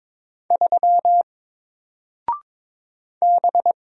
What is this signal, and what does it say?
Morse code, heard as a keyed tone, about 22 wpm, 700 Hz. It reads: 3 B
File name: Single-Letter-Number-22wpm